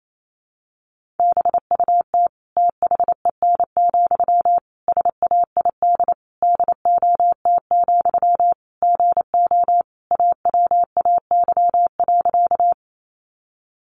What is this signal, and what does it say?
Morse code, heard as a keyed tone, about 28 wpm, 700 Hz. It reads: BUT THEN, SAID DOT, GO AWAY.
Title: Sentences from Top 1500 Words 28wpm